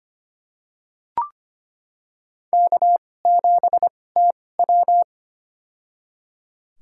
Morse code, heard as a keyed tone, about 25 wpm, spaced as for 18 wpm, 700 Hz. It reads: K7TW
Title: USA Call Signs - 4 Characters - 2x Character Spacing - 25wpm